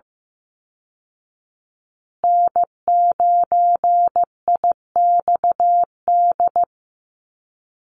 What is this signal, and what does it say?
Morse code, heard as a keyed tone, about 15 wpm, 700 Hz. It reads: N9IXD
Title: Call Signs Repeated 2x - 15wpm